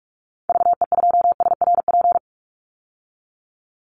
Morse code, heard as a keyed tone, about 44 wpm, 700 Hz. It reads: VE2SRP
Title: Call Signs - 45wpm